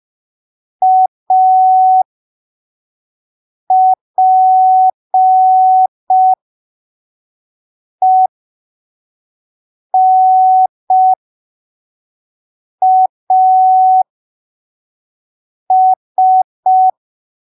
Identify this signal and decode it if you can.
Morse code, heard as a keyed tone, about 5 wpm, spaced as for 3 wpm, 750 Hz. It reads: APENAS
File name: arquivos / texto / texto 02 5ppm